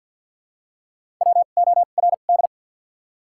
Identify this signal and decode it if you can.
Morse code, heard as a keyed tone, about 53 wpm, spaced as for 37 wpm, 700 Hz. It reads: WORD
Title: 2x Character Spacing - Top 200 Words - 50wpm